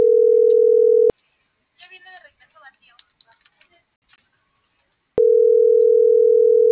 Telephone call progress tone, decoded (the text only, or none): ringback tone